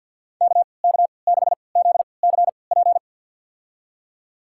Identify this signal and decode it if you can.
Morse code, heard as a keyed tone, about 51 wpm, spaced as for 28 wpm, 690 Hz. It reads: KK6ZCP
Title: USA Call Signs - 6 Characters - 3x Character Spacing - 50wpm